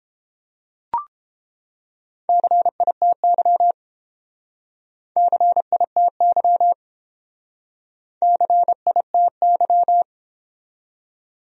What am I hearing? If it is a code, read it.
Morse code, heard as a keyed tone, about 27 wpm, 700 Hz. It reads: CITY CITY CITY
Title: Speed-Racing Top 500 Words 22wpm